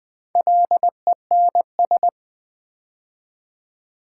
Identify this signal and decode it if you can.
Morse code, heard as a keyed tone, about 20 wpm, 700 Hz. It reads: LENS
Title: Build A Word - 2 Letters At A Time - 4-Letter Words - 20wpm